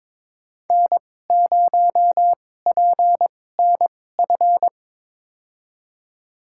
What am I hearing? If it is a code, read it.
Morse code, heard as a keyed tone, about 22 wpm, spaced as for 16 wpm, 700 Hz. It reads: N0PNF
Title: USA Call Signs - 5 Characters - 2x Character Spacing - 22wpm